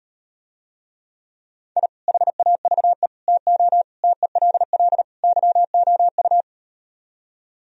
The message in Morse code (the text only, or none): I HAVE TO TELL YOU